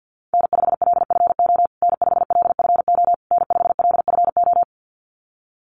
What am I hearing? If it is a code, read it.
Morse code, heard as a keyed tone, about 50 wpm, 700 Hz. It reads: N5LFO N5LFO N5LFO
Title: USA Call Signs Repeated 3x 50wpm